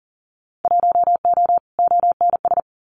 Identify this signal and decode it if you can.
Morse code, heard as a keyed tone, about 38 wpm, 700 Hz. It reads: 1O ONS
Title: Winter Field Day Exchanges Encore 40wpm